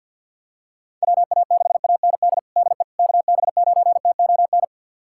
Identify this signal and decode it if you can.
Morse code, heard as a keyed tone, about 53 wpm, 700 Hz. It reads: WA6AND DE KB9TON
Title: USA Callsign DE Callsign 50wpm